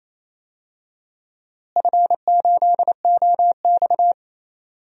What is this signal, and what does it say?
Morse code, heard as a keyed tone, about 27 wpm, 700 Hz. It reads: F8OX
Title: DXCC-100 Call Signs - 4 Characters - Repeated 2x - 28wpm